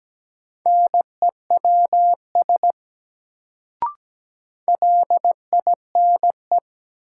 Morse code, heard as a keyed tone, about 17 wpm, 700 Hz. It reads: NEWS LINE
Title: Sets of 5 Words - New Word Drill - 17wpm